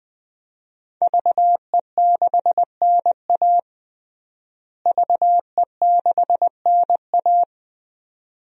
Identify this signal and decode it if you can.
Morse code, heard as a keyed tone, about 21 wpm, 700 Hz. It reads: VE6NA VE6NA
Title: DXCC-100 Call Signs - 5 Characters - Repeated 2x - 20wpm